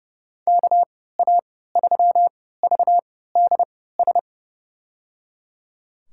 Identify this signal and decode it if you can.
Morse code, heard as a keyed tone, about 30 wpm, spaced as for 17 wpm, 700 Hz. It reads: KA3VDS
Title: USA Call Signs - 6 Characters - 3x Character Spacing - 30wpm